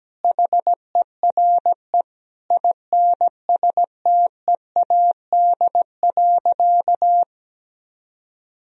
Morse code, heard as a keyed tone, about 17 wpm, 700 Hz. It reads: HERE INSTEAD.
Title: Sentences from Top 1000 Words - Review Part 4 - 17wpm